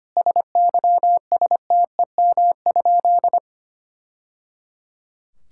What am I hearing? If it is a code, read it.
Morse code, heard as a keyed tone, about 25 wpm, 700 Hz. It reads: SYSTEM?
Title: Sentences from Top 1000 Words - Review Part 5 - 25wpm